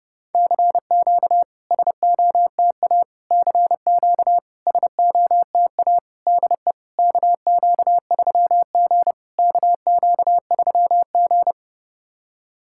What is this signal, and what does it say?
Morse code, heard as a keyed tone, about 32 wpm, 700 Hz. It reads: CQ SOTA CQ SOTA DE KQ3G KQ3G
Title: SOTA QSO Practice 30wpm